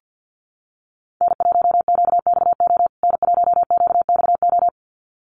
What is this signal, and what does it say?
Morse code, heard as a keyed tone, about 50 wpm, 700 Hz. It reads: N1QXO N1QXO
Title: USA Call Signs Repeated 2x 50wpm